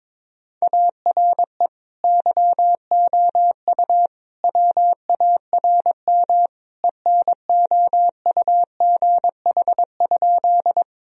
Morse code, heard as a keyed tone, about 22 wpm, 700 Hz. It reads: ARE YOU WARM ENOUGH?
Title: Sentences from Top 1000 Words - Review Part 2 - 22wpm